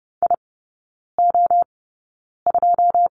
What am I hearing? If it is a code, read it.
Morse code, heard as a keyed tone, about 30 wpm, spaced as for 9 wpm, 700 Hz. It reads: IO2
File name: course / 30:10wpm / 087 Lesson 15e - Number 5 - Three Characters 30wpm